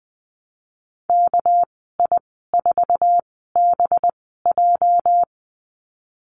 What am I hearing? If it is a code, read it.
Morse code, heard as a keyed tone, about 20 wpm, spaced as for 15 wpm, 700 Hz. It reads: KI4BJ
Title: 231 Lesson 37g - Letter x - Call Signs 20wpm